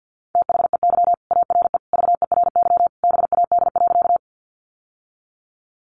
Morse code, heard as a keyed tone, about 50 wpm, 700 Hz. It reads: THEY ARE VERY BAD.
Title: Sentences from Top 700 Words 50wpm